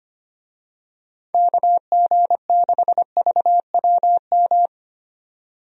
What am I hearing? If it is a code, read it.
Morse code, heard as a keyed tone, about 25 wpm, 700 Hz. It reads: KG6VWM
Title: USA Call Signs - 6 Characters - 25wpm